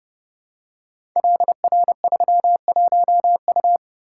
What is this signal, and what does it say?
Morse code, heard as a keyed tone, about 30 wpm, 700 Hz. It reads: LR31U
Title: Code Groups - 5 Letter:Numbers - 30wpm